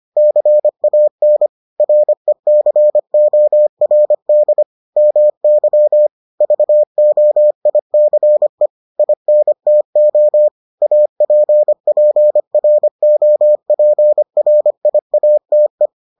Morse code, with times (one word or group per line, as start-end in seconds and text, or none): 0.2-1.5 CAN
1.8-4.6 RECORD
5.0-6.1 MY
6.4-8.7 VOICE
9.0-10.5 INTO
10.8-15.9 APPROPRIATE